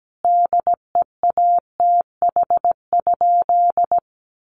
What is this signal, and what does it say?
Morse code, heard as a keyed tone, about 17 wpm, 700 Hz. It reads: DEATH?
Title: Sentences from Top 1100 Words 17wpm